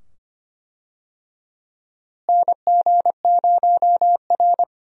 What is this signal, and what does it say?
Morse code, heard as a keyed tone, about 25 wpm, 700 Hz. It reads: NG0R
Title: USA Call Signs - 4 Characters - Repeated 2x - 25wpm